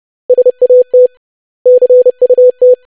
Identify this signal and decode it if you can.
Morse code, heard as a keyed tone, about 30 wpm, 500 Hz. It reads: SAT CUT
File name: lesson / web / basic500-25wpm